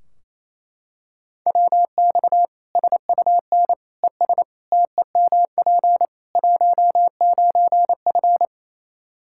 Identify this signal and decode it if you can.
Morse code, heard as a keyed tone, about 28 wpm, 700 Hz. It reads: WX SUN ES TEMP 19F
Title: Practice QSO - Exchange 2 - Info - 28wpm